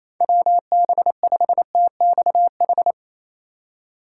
Morse code, heard as a keyed tone, about 27 wpm, 700 Hz. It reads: WB5TXH
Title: USA Call Signs - 6 Characters - 4x Character Spacing - 28wpm